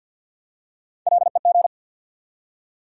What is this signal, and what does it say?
Morse code, heard as a keyed tone, about 53 wpm, 700 Hz. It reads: LEG
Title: Three Letter Words - 50wpm